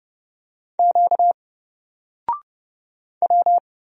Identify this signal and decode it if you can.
Morse code, heard as a keyed tone, about 30 wpm, 700 Hz. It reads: Q W
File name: Single-Letter-Number-30wpm